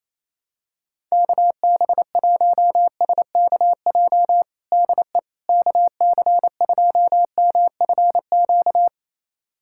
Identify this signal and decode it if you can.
Morse code, heard as a keyed tone, about 27 wpm, 700 Hz. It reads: KB1SKJ DE KC2MFQ
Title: USA Callsign DE Callsign 28wpm